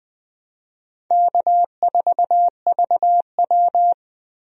Morse code, heard as a keyed tone, about 20 wpm, 700 Hz. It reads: K4VW
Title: USA Call Signs - 4 Characters - Repeated 2x - 20wpm